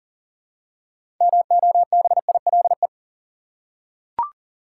Morse code, heard as a keyed tone, about 40 wpm, 700 Hz. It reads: MOBILE
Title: Six Letter Words - 40wpm